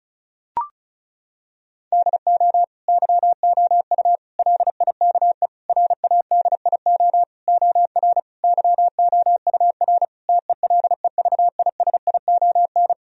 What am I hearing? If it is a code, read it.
Morse code, heard as a keyed tone, about 34 wpm, 700 Hz. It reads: DO YOU LIKE RADIO OR YOUR TELEVISION
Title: Sentences from Top 1000 Words - Review Part 4 - 35wpm